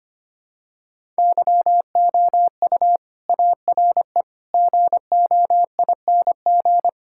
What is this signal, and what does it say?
Morse code, heard as a keyed tone, about 25 wpm, 700 Hz. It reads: YOU ARE GOING
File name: practice / 25wpm / Sentences with 4-Word Phrases - 25wpm